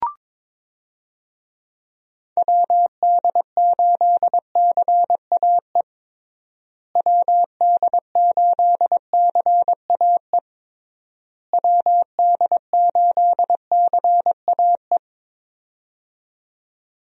Morse code, heard as a keyed tone, about 22 wpm, 700 Hz. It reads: WD8CAE WD8CAE WD8CAE
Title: USA Call Signs - 6 Characters - Repeated 3x - 22wpm